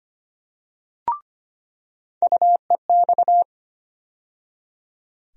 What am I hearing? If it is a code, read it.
Morse code, heard as a keyed tone, about 25 wpm, 700 Hz. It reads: UEX
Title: USA Call Sign Postfixes - 3 Letters 25wpm